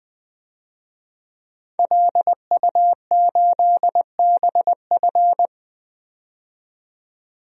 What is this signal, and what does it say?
Morse code, heard as a keyed tone, about 20 wpm, 700 Hz. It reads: LU8BF
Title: DXCC-100 Call Signs - 5 Characters - 20wpm